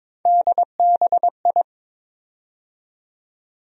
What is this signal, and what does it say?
Morse code, heard as a keyed tone, about 22 wpm, 700 Hz. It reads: DBI
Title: USA Call Sign Postfixes - 3 Letters 22wpm